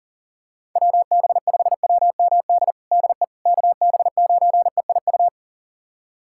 Morse code, heard as a keyed tone, about 40 wpm, 700 Hz. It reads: WB5WMD DE KB9EIU
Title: USA Callsign DE Callsign - 6 Characters - 40wpm